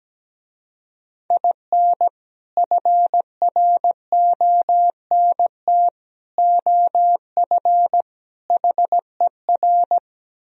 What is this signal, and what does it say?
Morse code, heard as a keyed tone, about 17 wpm, 700 Hz. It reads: IN FRONT OF HER
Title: Speed-Racing - Sets of 4 Words 17wpm